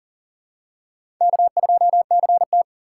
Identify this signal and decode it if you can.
Morse code, heard as a keyed tone, about 40 wpm, 700 Hz. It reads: K2CT
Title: USA Call Signs - 4x Character Spacing - 40wpm